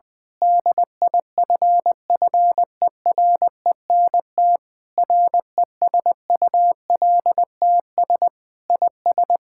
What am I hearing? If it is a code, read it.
Morse code, heard as a keyed tone, about 20 wpm, 700 Hz. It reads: DIFFERENT RESULTS IS
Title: Sentences with 9-Word Phrases - 20wpm